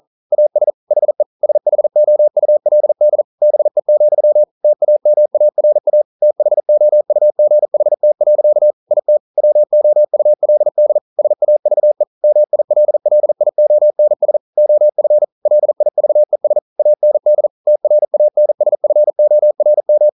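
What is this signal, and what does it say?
Morse code, heard as a keyed tone, about 40 wpm, 600 Hz. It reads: AS HE SHOULD BE, TAMARA THOUGHT. IT WOULD SAVE MILLIONS OF LIVES AND TRANSFORM
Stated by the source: Morse code at 40 wpm, practice text, CW med Gnister.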